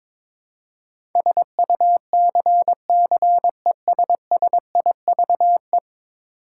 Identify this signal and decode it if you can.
Morse code, heard as a keyed tone, about 21 wpm, 700 Hz. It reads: SUCCESSIVE
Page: morse-code-ninja-practice-22wpm